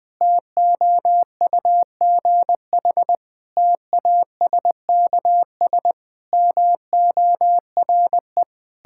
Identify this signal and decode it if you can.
Morse code, heard as a keyed tone, about 20 wpm, 700 Hz. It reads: TOUGH TASKS MORE
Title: Difficult Sentences with Hints - 20wpm